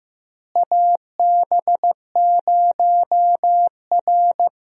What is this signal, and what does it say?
Morse code, heard as a keyed tone, about 15 wpm, 700 Hz. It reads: AB0R
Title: USA Call Signs - 4 Characters - Repeated 2x - 15wpm